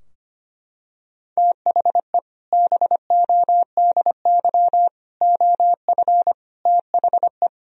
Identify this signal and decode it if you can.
Morse code, heard as a keyed tone, about 25 wpm, 700 Hz. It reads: THE BODY OF THE
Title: Sentences from Top 1000 Words - Review Part 2 - 25wpm